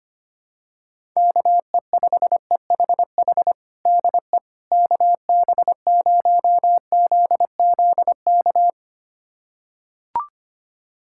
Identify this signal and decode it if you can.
Morse code, heard as a keyed tone, about 25 wpm, 700 Hz. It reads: KE5EHH DE KB0ZZK
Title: USA Callsign DE Callsign - 6 Characters - 25wpm